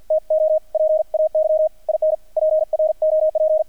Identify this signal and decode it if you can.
Morse code, heard as a keyed tone, about 48 wpm, 630 Hz, with radio noise. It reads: TO PAY IN PROP
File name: archived / CRSnet